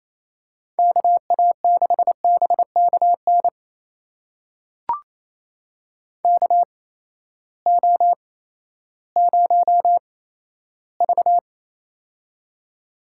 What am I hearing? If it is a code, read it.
Morse code, heard as a keyed tone, about 28 wpm, spaced as for 14 wpm, 700 Hz. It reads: KA6BKN KO0V
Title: USA Call Signs - 8x Character Spacing - 28wpm